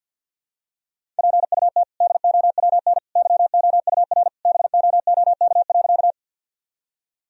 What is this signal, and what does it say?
Morse code, heard as a keyed tone, about 46 wpm, 700 Hz. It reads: PUT DOWN YOUR BOOK.
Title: Sentences from Top 1000 Words - Review Part 1 - 50wpm